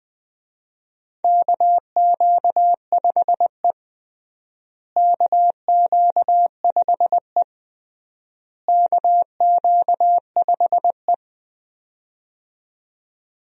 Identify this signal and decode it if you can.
Morse code, heard as a keyed tone, about 20 wpm, 700 Hz. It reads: KQ5E KQ5E KQ5E
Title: USA Call Signs - 4 Characters - Repeated 3x - 20wpm